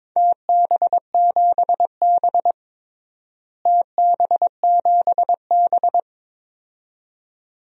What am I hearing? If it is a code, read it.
Morse code, heard as a keyed tone, about 22 wpm, 700 Hz. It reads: TB7B TB7B
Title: DXCC-100 Call Signs - 4 Characters - Repeated 2x - 22wpm